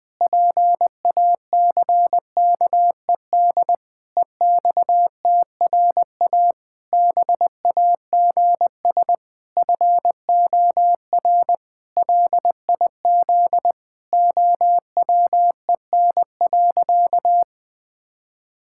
Morse code, heard as a keyed tone, about 21 wpm, 700 Hz. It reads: PACKED EXTRA BAGS FOR LIZ OWEN.
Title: Pangram Challenge v2 - 20wpm